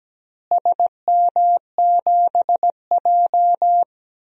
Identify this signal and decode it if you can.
Morse code, heard as a keyed tone, about 17 wpm, 700 Hz. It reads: SM7J
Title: DXCC-100 Call Signs - 4 Characters - Repeated 3x - 17wpm